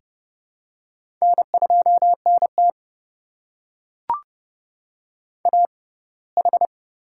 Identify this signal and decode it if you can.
Morse code, heard as a keyed tone, about 30 wpm, spaced as for 20 wpm, 700 Hz. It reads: N2NT AH
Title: USA Call Signs - 4 Characters - 6x Character Spacing - 30wpm